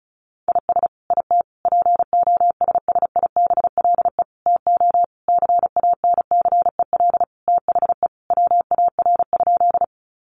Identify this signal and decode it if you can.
Morse code, heard as a keyed tone, about 34 wpm, 700 Hz. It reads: IS IT POSSIBLE TO CANCEL THE WAR?
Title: Sentences from Top 1200 Words 35wpm